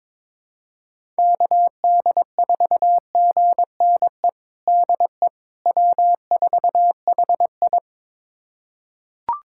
Morse code, heard as a keyed tone, about 22 wpm, 700 Hz. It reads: KD4GNE DE W4HI
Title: USA Callsign DE Callsign 22wpm